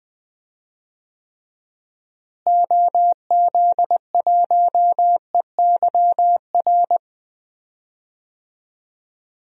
Morse code, heard as a keyed tone, about 20 wpm, 700 Hz. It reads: OZ1EYR
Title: DXCC-100 Call Signs - 6 Characters - Repeated 2x - 20wpm